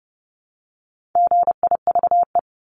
Morse code, heard as a keyed tone, about 30 wpm, 700 Hz. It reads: GIVE
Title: Multi-letter Phonemes Repeated 3x 30wpm